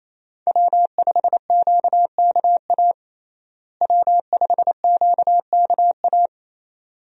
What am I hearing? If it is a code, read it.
Morse code, heard as a keyed tone, about 28 wpm, 700 Hz. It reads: W5QKA W5QKA
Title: USA Call Signs - 5 Characters - Repeated 2x - 28wpm